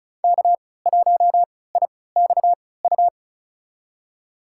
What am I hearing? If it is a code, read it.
Morse code, heard as a keyed tone, about 34 wpm, spaced as for 20 wpm, 700 Hz. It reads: K1IXU
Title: USA Call Signs - 5 Characters - 3x Character Spacing - 35wpm